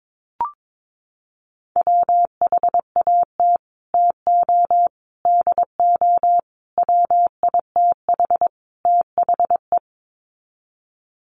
Morse code, heard as a keyed tone, about 22 wpm, 700 Hz. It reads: WHAT TO DO WITH THE
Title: Sets of 5 Words Encore2 22wpm